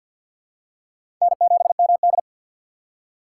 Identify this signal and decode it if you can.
Morse code, heard as a keyed tone, about 52 wpm, 700 Hz. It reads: N7MD
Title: USA Call Signs - 4 Characters - 6x Character Spacing - 50wpm